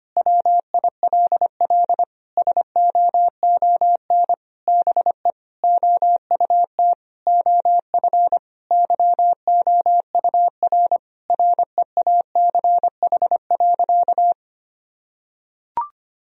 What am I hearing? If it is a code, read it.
Morse code, heard as a keyed tone, about 25 wpm, 700 Hz. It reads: WILL SOON BE OUT OF YOUR REACH.
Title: Sentences from Top 1000 Words - Review Part 2 - 25wpm